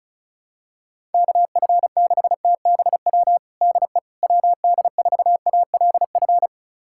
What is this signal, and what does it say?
Morse code, heard as a keyed tone, about 34 wpm, 700 Hz. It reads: KF6TBW DE WD4ALF
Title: USA Callsign DE Callsign - 6 Characters - 35wpm